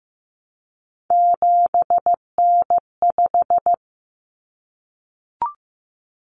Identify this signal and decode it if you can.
Morse code, heard as a keyed tone, about 15 wpm, 700 Hz. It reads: 7N5
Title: Common Cut Numbers - 3-Digit 15wpm